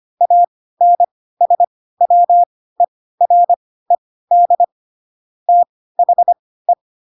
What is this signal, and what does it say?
Morse code, heard as a keyed tone, about 25 wpm, spaced as for 16 wpm, 700 Hz. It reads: ANSWERED THE